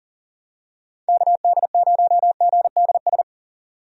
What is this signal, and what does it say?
Morse code, heard as a keyed tone, about 40 wpm, 700 Hz. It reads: KD0GDS
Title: USA Call Signs - 6 Characters - 40wpm